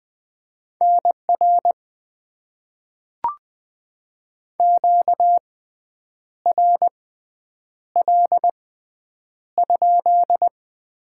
Morse code, heard as a keyed tone, about 20 wpm, spaced as for 8 wpm, 700 Hz. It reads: NR QRL?
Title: Top 90 QSO Elements - 6x Character Spacing - 20wpm